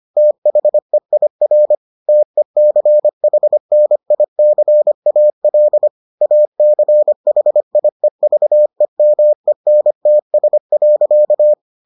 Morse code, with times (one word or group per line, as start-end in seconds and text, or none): 0.2-1.8 THEIR
2.1-5.9 TECHNICAL
6.2-11.5 ACHIEVEMENTS.